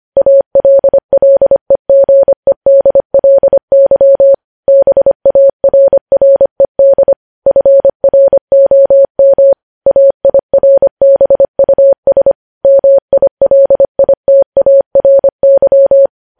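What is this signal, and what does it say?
Morse code, heard as a keyed tone, about 25 wpm, 550 Hz. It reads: ALLEGEDLY BARRED FROM AIRBUS MILITARY
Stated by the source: News headlines sent at 25.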